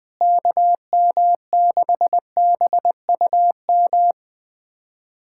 Morse code, heard as a keyed tone, about 21 wpm, 700 Hz. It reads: KM6BUM
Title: USA Callsign DE Callsign 20wpm